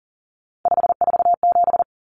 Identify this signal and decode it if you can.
Morse code, heard as a keyed tone, about 40 wpm, 700 Hz. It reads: 547
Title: Common Cut Numbers - 3-Digit 40wpm